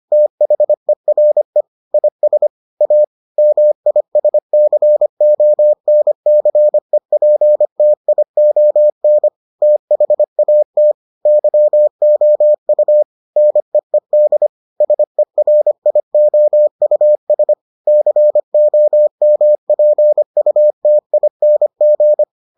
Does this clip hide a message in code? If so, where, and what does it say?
Yes: Morse code from 0.1 to 22.2 s, heard as a keyed tone, about 25 wpm, 600 Hz. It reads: THERE IS A MISCONCEPTION THAT YOU NEED SERIOUS COMPUTING